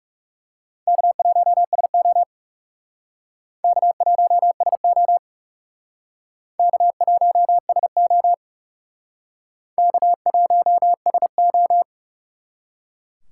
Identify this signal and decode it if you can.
Morse code, heard as a keyed tone, about 37 wpm, 700 Hz. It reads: K1SO K1SO K1SO K1SO
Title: Speed Racing - USA 1x2 Call Signs 30wpm